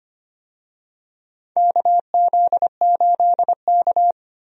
Morse code, heard as a keyed tone, about 25 wpm, 700 Hz. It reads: KZ8K
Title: USA Call Signs - 4 Characters - Repeated 3x - 25wpm